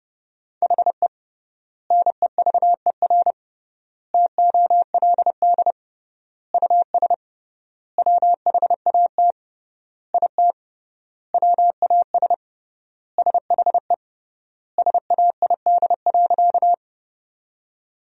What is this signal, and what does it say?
Morse code, heard as a keyed tone, about 31 wpm, 700 Hz. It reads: HE NEVER TOLD US WHAT IT WAS SHE SAID.
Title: 3x Word-Spacing - Sentences from Top 400 Words - 30wpm